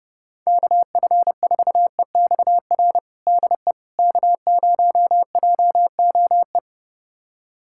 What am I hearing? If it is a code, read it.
Morse code, heard as a keyed tone, about 32 wpm, 700 Hz. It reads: KF4EXR DE K0JOE
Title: USA Callsign DE Callsign 30wpm